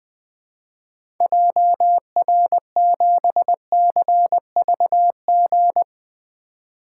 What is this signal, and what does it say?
Morse code, heard as a keyed tone, about 21 wpm, 700 Hz. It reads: JR7CVG
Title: 2023 POTA Activator Call Signs - 20wpm